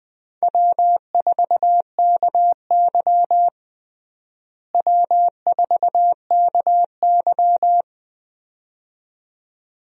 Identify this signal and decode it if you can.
Morse code, heard as a keyed tone, about 20 wpm, 700 Hz. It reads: W4KY W4KY
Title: USA Call Signs - 4 Characters - Repeated 2x - 20wpm